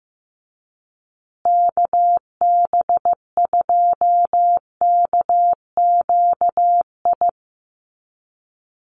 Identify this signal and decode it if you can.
Morse code, heard as a keyed tone, about 15 wpm, 700 Hz. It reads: KB2KQI
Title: USA Call Signs Repeated 3x 15wpm